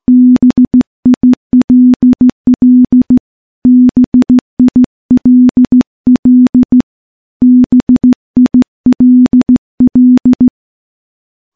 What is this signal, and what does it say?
Morse code, heard as a keyed tone, about 15 wpm, 250 Hz. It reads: BILL BILL BILL
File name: morse / morse-training-15WPM